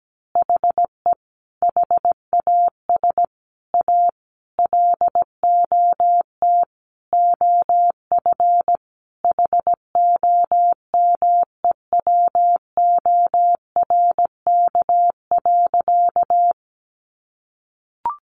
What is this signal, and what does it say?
Morse code, heard as a keyed tone, about 17 wpm, 700 Hz. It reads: HE HAS A LOT OF HOMEWORK.
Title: Additional Easy Phrases 17wpm